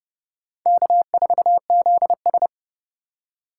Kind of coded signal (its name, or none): Morse code